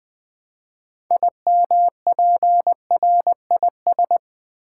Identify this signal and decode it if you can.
Morse code, heard as a keyed tone, about 20 wpm, 700 Hz. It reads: IMPRIS
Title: Build A Word - 3 Letters At A Time - 8-Letter Words - 20wpm